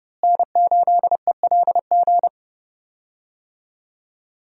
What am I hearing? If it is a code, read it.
Morse code, heard as a keyed tone, about 29 wpm, 700 Hz. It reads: N8ELG
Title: USA Call Signs - 5 Characters - Repeated 3x - 30wpm